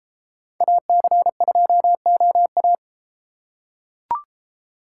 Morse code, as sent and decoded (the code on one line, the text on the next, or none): .- -.-. ..--- --- .-
AC2OA